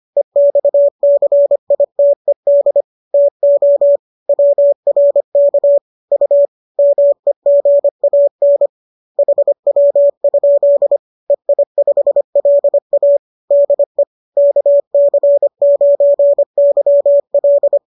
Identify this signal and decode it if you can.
Morse code, heard as a keyed tone, about 25 wpm, 570 Hz. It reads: EXCITED TO WRK U MEGAN HW? EI5LA DE KC9YL